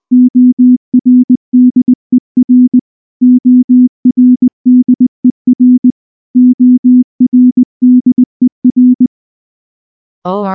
Morse code, heard as a keyed tone, about 19 wpm, 250 Hz. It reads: ORDER ORDER ORDER
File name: morse / morse-training-20WPM-3